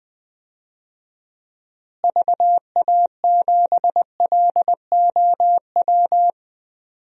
Morse code, heard as a keyed tone, about 20 wpm, 700 Hz. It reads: VA7LOW
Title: DXCC-100 Call Signs - 6 Characters - Repeated 3x - 20wpm